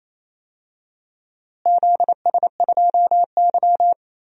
Morse code, heard as a keyed tone, about 28 wpm, 700 Hz. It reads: ZS2Y
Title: DXCC-100 Call Signs - 4 Characters - Repeated 3x - 28wpm